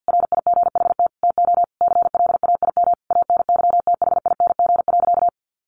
Morse code, heard as a keyed tone, about 50 wpm, 700 Hz. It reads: RIGHT TO CLAIM ANYTHING.
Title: Sentences from Top 1500 Words 50wpm